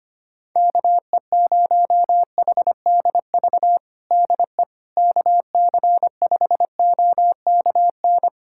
Morse code, heard as a keyed tone, about 25 wpm, 700 Hz. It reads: KE0HDV DE KC5OKN
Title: USA Callsign DE Callsign 25wpm